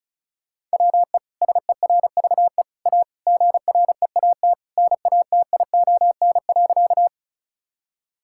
Morse code, heard as a keyed tone, about 36 wpm, 700 Hz. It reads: WE SERVE A GREAT NATION.
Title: Sentences from Top 1000 Words - Review Part 6 - 35wpm